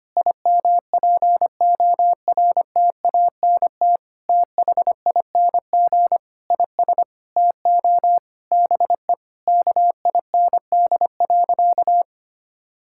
Morse code, heard as a keyed tone, about 25 wpm, 700 Hz. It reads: IMPORTANT THING IS TO BE KIND.
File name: practice / 25wpm / Sentences with 4-Word Phrases - 25wpm